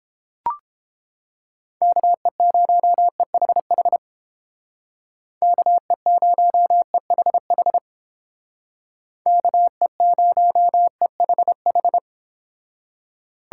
Morse code, heard as a keyed tone, about 30 wpm, 700 Hz. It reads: KE0EHH KE0EHH KE0EHH